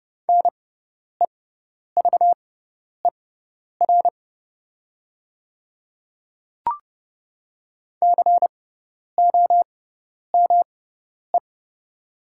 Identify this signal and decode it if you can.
Morse code, heard as a keyed tone, about 30 wpm, spaced as for 10 wpm, 700 Hz. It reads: NEVER COME
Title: 6x Character Spacing - Top 300 Words - 30wpm